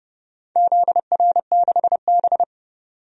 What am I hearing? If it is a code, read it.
Morse code, heard as a keyed tone, about 29 wpm, 700 Hz. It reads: ZR6B
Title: DXCC-100 Call Signs - 4 Characters - 30wpm